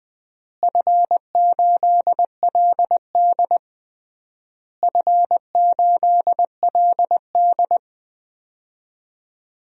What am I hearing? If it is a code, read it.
Morse code, heard as a keyed tone, about 20 wpm, 700 Hz. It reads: F8LD F8LD
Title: DXCC-100 Call Signs - 4 Characters - Repeated 3x - 20wpm